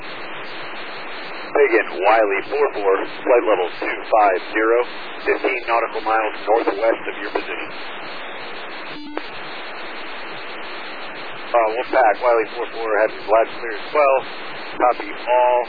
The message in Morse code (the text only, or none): none